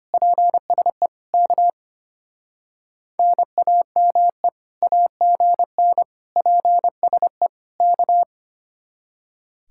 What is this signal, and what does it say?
Morse code, heard as a keyed tone, about 31 wpm, 700 Hz. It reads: PSE K NAME AGN PSE K
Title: Speed-Racing Common QSO Phrases Encore - 25wpm